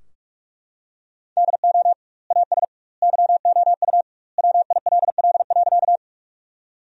Morse code, heard as a keyed tone, about 44 wpm, 700 Hz. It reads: DO AS YOU WILL.
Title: 2x Word-Spacing - Sentences from Top 100 Words - 45wpm